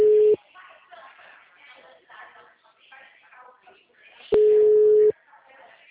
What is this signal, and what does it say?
Ringback tone (Europe, 425 Hz), with speech.
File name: out- -2015- - - .45343